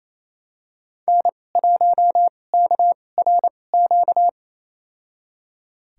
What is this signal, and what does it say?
Morse code, heard as a keyed tone, about 28 wpm, spaced as for 20 wpm, 700 Hz. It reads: N1KRQ
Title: USA Call Signs - 5 Characters - 2x Character Spacing - 28wpm